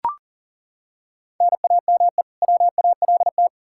Morse code, heard as a keyed tone, about 40 wpm, 700 Hz. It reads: NAME WALT
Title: Practice QSO - Exchange 1 - Beginning - 40wpm